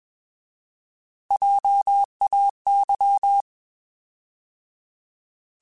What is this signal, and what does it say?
Morse code, heard as a keyed tone, about 21 wpm, 800 Hz. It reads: JAY